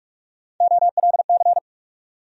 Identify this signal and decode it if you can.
Morse code, heard as a keyed tone, about 46 wpm, 700 Hz. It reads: OLC
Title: USA Call Sign Postfixes - 3 Letters 45wpm